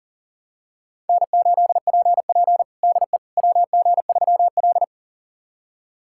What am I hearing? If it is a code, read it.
Morse code, heard as a keyed tone, about 37 wpm, 700 Hz. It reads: N8PP DE WG3L
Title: USA Callsign DE Callsign - 4 Characters - 40wpm